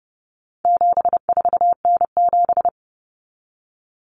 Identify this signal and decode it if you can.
Morse code, heard as a keyed tone, about 30 wpm, 700 Hz. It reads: 74N7
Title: Common Cut Numbers - 4-Digit 30wpm